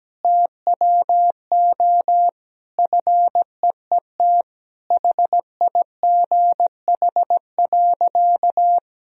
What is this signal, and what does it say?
Morse code, heard as a keyed tone, about 17 wpm, 700 Hz. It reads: TWO FEET HIGH.
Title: Sentences from Top 1000 Words - Review Part 1 - 17wpm